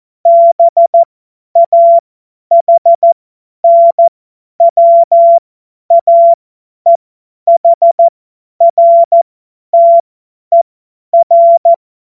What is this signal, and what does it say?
Morse code, heard as a keyed tone, about 14 wpm, spaced as for 10 wpm, 680 Hz. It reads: BAHNWAEHRTER
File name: Morsekurs / Witze / Witz 20 Februar 2024 50 F70